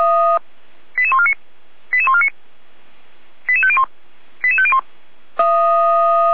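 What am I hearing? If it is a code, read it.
Five-tone selective call (ZVEI), heard as quick runs of five tones, tones of about 70 ms. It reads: 88149 88149 88501 88501
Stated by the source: ZVEI five-tone call received on an SDR, from the feuerwehr-orlamuende.org archive.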